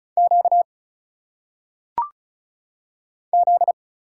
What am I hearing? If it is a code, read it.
Morse code, heard as a keyed tone, about 34 wpm, 700 Hz. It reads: Q Z
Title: Single Letters - Phonetic Pronunciation 35wpm